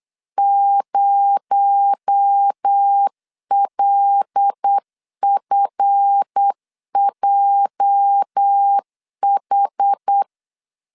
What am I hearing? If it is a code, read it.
Morse code, heard as a keyed tone, about 8 wpm, 800 Hz. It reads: 0LFJH
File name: LEZIONI / 8 YJ45 / 8 YJ 8wpm 3313